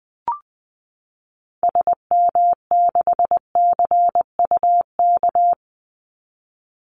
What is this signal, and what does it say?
Morse code, heard as a keyed tone, about 20 wpm, 700 Hz. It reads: SM6CUK
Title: Call Signs Repeated 2x - 20wpm